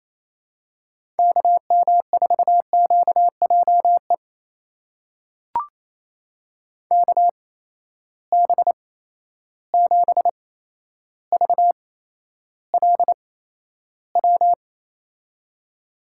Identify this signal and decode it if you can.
Morse code, heard as a keyed tone, about 28 wpm, spaced as for 12 wpm, 700 Hz. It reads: KM4QJE KB7VLW
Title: USA Call Signs - 6 Characters - 8x Character Spacing - 28wpm